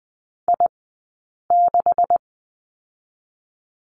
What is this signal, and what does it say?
Morse code, heard as a keyed tone, about 20 wpm, spaced as for 8 wpm, 700 Hz. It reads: I6
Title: 126 Lesson 21d - Number 6 - Two Characters 20wpm